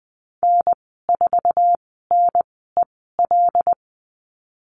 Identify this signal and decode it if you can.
Morse code, heard as a keyed tone, about 20 wpm, spaced as for 15 wpm, 700 Hz. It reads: N4NEL
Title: 076 Lesson 13g - Letter L - Call Signs 20wpm